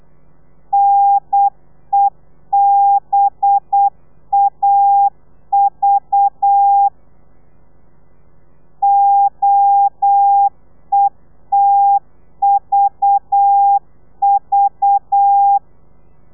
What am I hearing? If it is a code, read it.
Morse code, heard as a keyed tone, about 8 wpm, 800 Hz. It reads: NEBAV OETVV
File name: telegrafia / i2tfj / Lez4 35